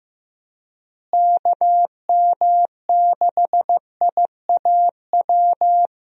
Morse code, heard as a keyed tone, about 15 wpm, 700 Hz. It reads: KM6IAW
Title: USA Call Signs - 6 Characters - Repeated 2x - 15wpm